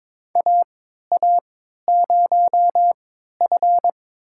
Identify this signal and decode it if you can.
Morse code, heard as a keyed tone, about 22 wpm, spaced as for 13 wpm, 700 Hz. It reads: AA0F
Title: USA Call Signs - 3x Character Spacing - 22wpm